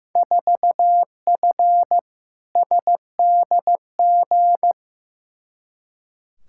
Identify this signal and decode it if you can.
Morse code, heard as a keyed tone, about 15 wpm, 700 Hz. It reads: 4F SDG
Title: Field Day Exchanges - 15wpm